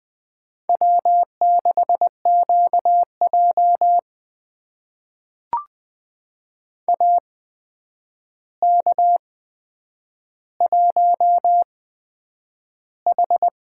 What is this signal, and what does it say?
Morse code, heard as a keyed tone, about 20 wpm, spaced as for 9 wpm, 700 Hz. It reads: W6QJ AK1H
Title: USA Call Signs - 8x Character Spacing - 20wpm